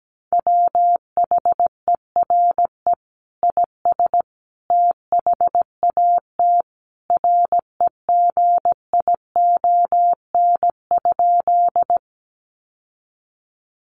Morse code, heard as a keyed tone, about 17 wpm, 700 Hz. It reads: WHERE IS THAT REGION?
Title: Sentences from Top 1500 Words 17wpm